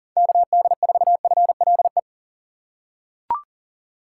Morse code, heard as a keyed tone, about 40 wpm, 700 Hz. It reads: KD4FLE
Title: USA Call Signs - 6 Characters - 2x Character Spacing - 40wpm